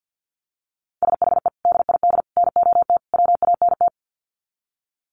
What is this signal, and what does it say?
Morse code, heard as a keyed tone, about 50 wpm, 700 Hz. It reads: SHE DID NOT WANT
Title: Sets of 4 Words 50wpm